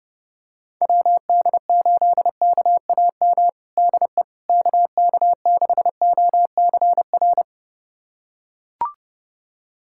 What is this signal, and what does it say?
Morse code, heard as a keyed tone, about 30 wpm, 700 Hz. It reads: WD8KAM DE KK6OCR
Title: USA Callsign DE Callsign 30wpm